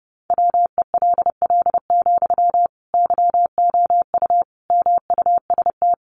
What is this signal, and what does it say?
Morse code, heard as a keyed tone, about 28 wpm, 700 Hz. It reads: WELL, YOU MUST
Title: Sentences from Top 700 Words 30wpm